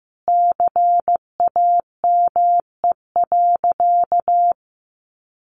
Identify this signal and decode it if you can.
Morse code, heard as a keyed tone, about 15 wpm, 700 Hz. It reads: CAME.
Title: Sentences from Top 1500 Words 15wpm